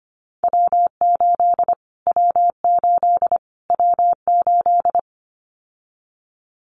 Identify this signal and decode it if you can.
Morse code, heard as a keyed tone, about 25 wpm, 700 Hz. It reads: W8 W8 W8
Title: Callsign Prefixes for Top 25 DXCC - Repeated 3x - 25wpm